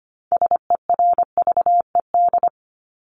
Morse code, heard as a keyed tone, about 25 wpm, 700 Hz. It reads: SERVED
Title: ED Suffix - Top 100 Words with Base Words 25wpm